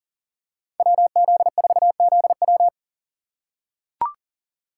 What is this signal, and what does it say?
Morse code, heard as a keyed tone, about 40 wpm, 700 Hz. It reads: WZ4ZW
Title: USA Call Signs - 5 Characters - 6x Character Spacing - 40wpm